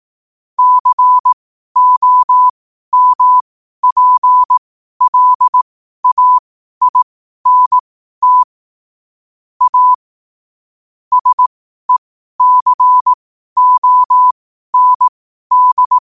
Morse code, heard as a keyed tone, about 18 wpm, spaced as for 12 wpm, 1000 Hz. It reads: COMPLAINT A SECOND
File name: qotdfiles / 12wpm